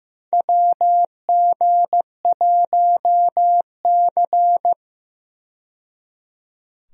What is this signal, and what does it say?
Morse code, heard as a keyed tone, about 15 wpm, 700 Hz. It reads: WG1C
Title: USA Call Signs - 4 Characters - 15wpm